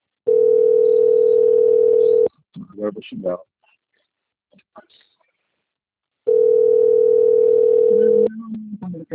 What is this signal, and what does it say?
Ringback tone (North America), 440 + 480 Hz, with speech.
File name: out- - - - - .81908